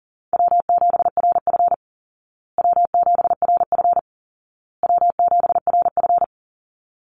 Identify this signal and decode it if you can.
Morse code, heard as a keyed tone, about 40 wpm, 700 Hz. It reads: W7RF W7RF W7RF
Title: Easy Call Signs Repeated 3x - 40wpm